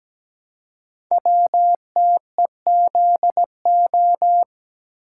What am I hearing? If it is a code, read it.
Morse code, heard as a keyed tone, about 17 wpm, 700 Hz. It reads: WTEZO
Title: Code Groups - 5 Letters - 17wpm